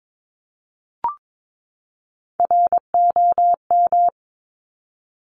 Morse code, heard as a keyed tone, about 22 wpm, 700 Hz. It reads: ROM
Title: Top Three Letter Combinations in Words 22wpm